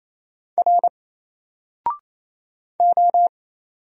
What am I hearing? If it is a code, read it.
Morse code, heard as a keyed tone, about 27 wpm, 700 Hz. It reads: R O
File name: Single-Letter-Number-28wpm